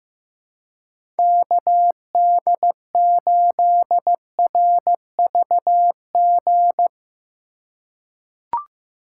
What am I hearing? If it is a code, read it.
Morse code, heard as a keyed tone, about 15 wpm, 700 Hz. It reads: KD8RVG
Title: USA Call Signs - 6 Characters - 4x Character Spacing - 15wpm